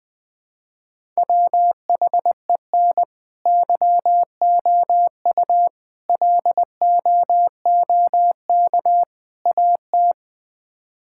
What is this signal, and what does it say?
Morse code, heard as a keyed tone, about 21 wpm, 700 Hz. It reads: WHEN YOU LOOK AT
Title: Build A Phrase - 5-Word Phrases - 20wpm